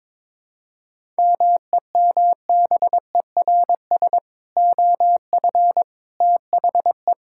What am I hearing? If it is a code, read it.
Morse code, heard as a keyed tone, about 21 wpm, 700 Hz. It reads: MEMBERS OF THE
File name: practice / 22wpm / Sentences with 3-Word Phrases - 22wpm